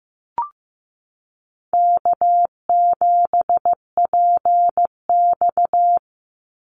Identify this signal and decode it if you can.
Morse code, heard as a keyed tone, about 15 wpm, 700 Hz. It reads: K7PX
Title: NA SOTA Chaser Callsigns 15wpm